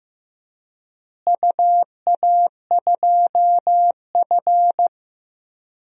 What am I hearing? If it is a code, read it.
Morse code, heard as a keyed tone, about 15 wpm, 700 Hz. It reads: UA2F
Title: DXCC-100 Call Signs - 4 Characters - Repeated 3x - 15wpm